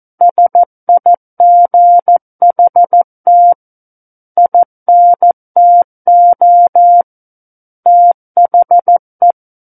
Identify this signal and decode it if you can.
Morse code, heard as a keyed tone, about 14 wpm, 700 Hz. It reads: SIGHT INTO THE
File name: Book One / 05-Chapter 5 - 14 WPM